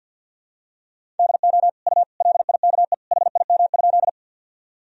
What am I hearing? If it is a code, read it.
Morse code, heard as a keyed tone, about 52 wpm, 700 Hz. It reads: DO U LIKE HIM?
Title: Sentences from Top 100 Words and Abbreviations 50wpm